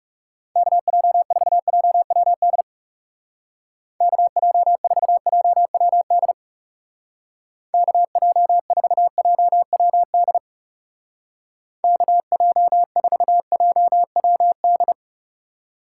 Morse code, heard as a keyed tone, about 38 wpm, 700 Hz. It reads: KJ4JWD KJ4JWD KJ4JWD KJ4JWD